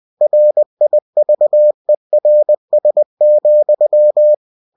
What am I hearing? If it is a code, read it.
Morse code, heard as a keyed tone, about 20 wpm, 600 Hz. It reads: RIVERS,
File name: cw-books / winnie-the-pooh / winnie-20wpm-page-0091